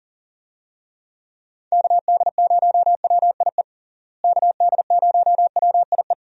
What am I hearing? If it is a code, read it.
Morse code, heard as a keyed tone, about 37 wpm, 700 Hz. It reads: KD0WIE KD0WIE
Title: USA Call Signs - 6 Characters - Repeated 2x - 40wpm